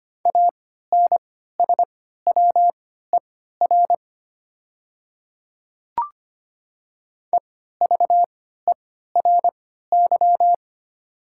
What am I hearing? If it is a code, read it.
Morse code, heard as a keyed tone, about 25 wpm, spaced as for 14 wpm, 700 Hz. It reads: ANSWER EVERY
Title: 3x Character Spacing - Top 300 Words - 25wpm